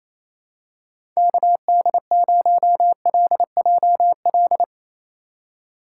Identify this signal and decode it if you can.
Morse code, heard as a keyed tone, about 28 wpm, 700 Hz. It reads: KD0LJL
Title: USA Call Signs - 6 Characters - 3x Character Spacing - 28wpm